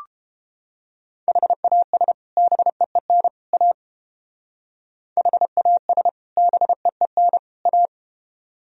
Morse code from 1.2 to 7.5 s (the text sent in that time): HAS BEEN A HAS BEEN